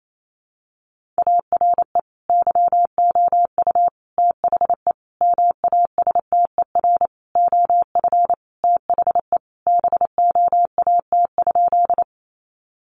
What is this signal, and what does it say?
Morse code, heard as a keyed tone, about 28 wpm, 700 Hz. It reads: ARE YOU THE MASTER OF THE BOAT?
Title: Sentences from Top 1300 Words 28wpm